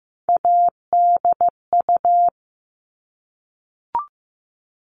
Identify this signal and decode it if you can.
Morse code, heard as a keyed tone, about 15 wpm, 700 Hz. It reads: ADU
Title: All Cut Numbers - 3-Digit 15wpm